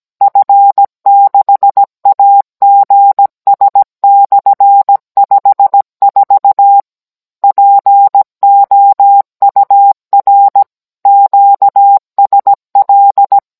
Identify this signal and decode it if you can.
Morse code, heard as a keyed tone, about 17 wpm, 800 Hz. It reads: F6AGS/54 POUR QSL
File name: Cours-NR19-16mm